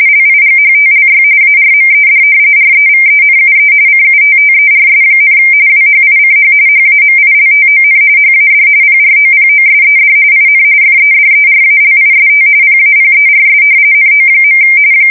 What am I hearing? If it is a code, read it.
RTTY, about 45.45 baud. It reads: LIGHTLY ARMED REVENUE CUTTERS, WHICH HAMILTON HAD PROPOSED IN FEDERALIST PAPERS NO.
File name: fsk170 / usnavy1